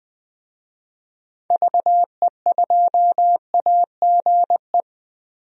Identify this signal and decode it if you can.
Morse code, heard as a keyed tone, about 20 wpm, 700 Hz. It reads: VE2AGE
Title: DXCC-100 Call Signs - 6 Characters - Repeated 3x - 20wpm